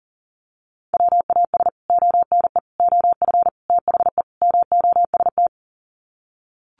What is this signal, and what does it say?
Morse code, heard as a keyed tone, about 40 wpm, 700 Hz. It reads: WAS ONE OF THE MOST
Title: Sets of 5 Words 40wpm